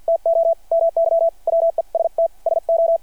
Morse code, heard as a keyed tone, about 48 wpm, spaced as for 42 wpm, 660 Hz. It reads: TO MY WEST SO
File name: archived / CRSnet